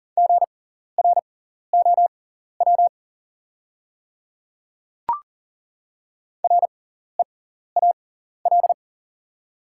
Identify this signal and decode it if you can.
Morse code, heard as a keyed tone, about 40 wpm, spaced as for 14 wpm, 700 Hz. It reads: GROW REAL
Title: 6x Character Spacing - Top 300 Words - 40wpm